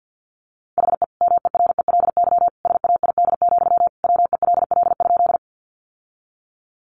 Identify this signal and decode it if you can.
Morse code, heard as a keyed tone, about 50 wpm, 700 Hz. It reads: HE MERELY SAID, WELL?
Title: Sentences from Top 1400 Words 50wpm